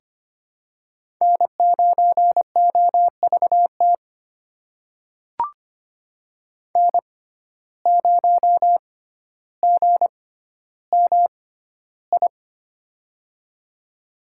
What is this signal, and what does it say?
Morse code, heard as a keyed tone, about 25 wpm, spaced as for 13 wpm, 700 Hz. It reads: N9OVT N0GMI
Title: USA Call Signs - 5 Characters - 6x Character Spacing - 25wpm